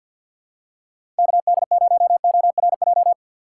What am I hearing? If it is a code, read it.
Morse code, heard as a keyed tone, about 52 wpm, 690 Hz. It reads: KD0ORJ
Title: USA Call Signs - 6x Character Spacing - 50wpm